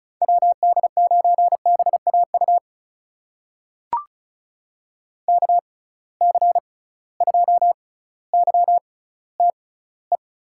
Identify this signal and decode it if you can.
Morse code, heard as a keyed tone, about 36 wpm, spaced as for 18 wpm, 700 Hz. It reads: WD9BAU KC2YTE
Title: USA Call Signs - 6 Characters - 6x Character Spacing - 35wpm